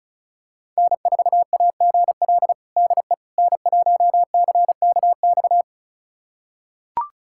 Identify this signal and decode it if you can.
Morse code, heard as a keyed tone, about 36 wpm, 700 Hz. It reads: N4AGL DE N1CKX
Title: USA Callsign DE Callsign - 5 Characters - 35wpm